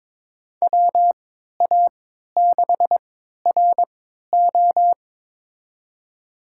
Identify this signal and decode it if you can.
Morse code, heard as a keyed tone, about 22 wpm, spaced as for 13 wpm, 700 Hz. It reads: WA6RO
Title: USA Call Signs - 5 Characters - 3x Character Spacing - 22wpm